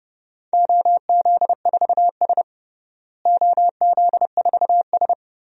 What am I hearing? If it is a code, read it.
Morse code, heard as a keyed tone, about 30 wpm, 700 Hz. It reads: OZ4S OZ4S
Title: DXCC-100 Call Signs - 4 Characters - Repeated 3x - 30wpm